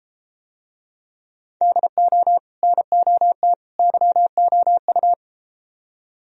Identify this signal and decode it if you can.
Morse code, heard as a keyed tone, about 33 wpm, 700 Hz. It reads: DO NOT YOU